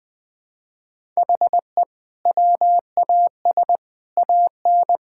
Morse code, heard as a keyed tone, about 21 wpm, 700 Hz. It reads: HE WAS AN
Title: Sentences from Top 1000 Words - Review Part 6 - 20wpm